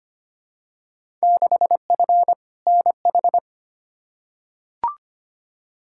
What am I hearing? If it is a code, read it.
Morse code, heard as a keyed tone, about 25 wpm, 700 Hz. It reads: 6F NH
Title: Field Day Exchanges Encore - 25wpm